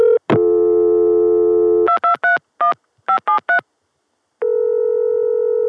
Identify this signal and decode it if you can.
DTMF, heard as telephone touch tones, tones of about 110 ms. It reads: 2231 573